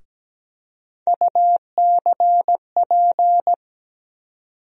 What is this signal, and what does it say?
Morse code, heard as a keyed tone, about 17 wpm, 700 Hz. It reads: UCP
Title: USA Call Sign Postfixes - 3 Letters 17wpm